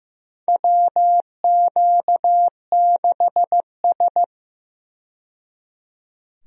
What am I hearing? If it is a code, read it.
Morse code, heard as a keyed tone, about 15 wpm, 700 Hz. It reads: WQ6S
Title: USA Call Signs - 4 Characters - 15wpm